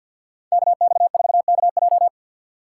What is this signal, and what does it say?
Morse code, heard as a keyed tone, about 49 wpm, 700 Hz. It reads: KX4KJ
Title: USA Call Signs - 5 Characters - 2x Character Spacing - 50wpm